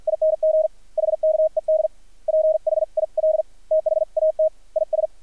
Morse code, heard as a keyed tone, about 50 wpm, spaced as for 44 wpm, 630 Hz, with radio noise. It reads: ING HKED PHIL THAT IS
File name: archived / CRSnet